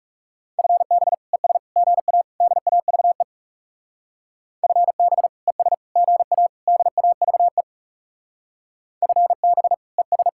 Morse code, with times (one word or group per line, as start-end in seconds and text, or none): 0.6-1.1 FB
1.3-1.6 ES
1.8-2.2 GA
2.4-3.2 DAVE
4.6-5.3 FB
5.5-5.7 ES
6.0-6.5 GA
6.7-7.6 DAVE
9.0-9.7 FB
10.0-10.3 ES